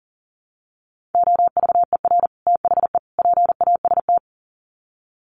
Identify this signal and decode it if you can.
Morse code, heard as a keyed tone, about 40 wpm, 700 Hz. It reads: OVER THE PAST
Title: Sets of 3 Words 40wpm